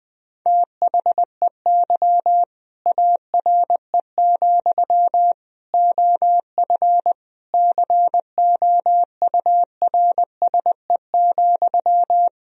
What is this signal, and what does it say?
Morse code, heard as a keyed tone, about 20 wpm, 700 Hz. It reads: THEY ARE, OF COURSE,
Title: Sentences from Top 1000 Words - Review Part 4 - 20wpm